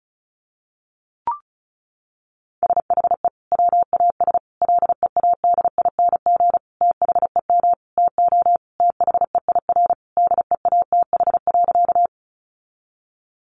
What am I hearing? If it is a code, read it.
Morse code, heard as a keyed tone, about 34 wpm, 700 Hz. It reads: SHE WAS LEADING THEM TO THEIR DEATH.
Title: Sentences from Top 1100 Words 35wpm